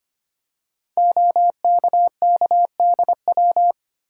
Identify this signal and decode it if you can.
Morse code, heard as a keyed tone, about 25 wpm, 700 Hz. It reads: OKKDW
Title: Code Groups - 5 Letter:Numbers - 25wpm